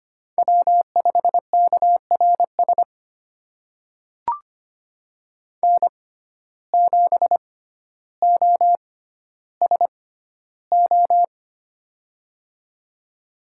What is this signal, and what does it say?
Morse code, heard as a keyed tone, about 25 wpm, spaced as for 13 wpm, 700 Hz. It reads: W5KRS N7OSO
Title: USA Call Signs - 5 Characters - 6x Character Spacing - 25wpm